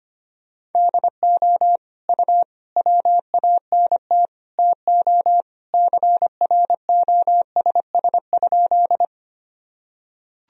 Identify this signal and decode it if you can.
Morse code, heard as a keyed tone, about 25 wpm, 700 Hz. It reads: DO U WANT TO CROSS?
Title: Sentences from Top 300 Words and Abbreviations 25wpm